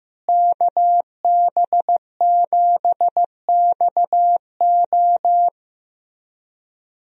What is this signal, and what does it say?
Morse code, heard as a keyed tone, about 15 wpm, 700 Hz. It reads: KB7XO
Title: USA Call Signs - 5 Characters - Repeated 3x - 15wpm